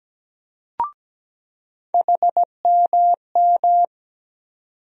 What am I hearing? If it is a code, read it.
Morse code, heard as a keyed tone, about 17 wpm, 700 Hz. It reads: HMM
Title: USA Call Sign Postfixes - 3 Letters 17wpm